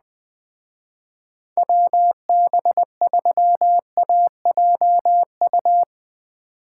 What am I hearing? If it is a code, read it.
Morse code, heard as a keyed tone, about 20 wpm, 700 Hz. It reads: WB3AJU
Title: Speed Racing - USA 2x3 Call Signs 17wpm